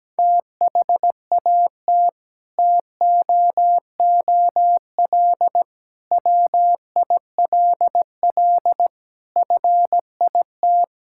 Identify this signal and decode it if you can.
Morse code, heard as a keyed tone, about 17 wpm, 700 Hz. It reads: THAT TOOL WILL FIT
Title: Sentences from Top 1000 Words - Review Part 6 - 17wpm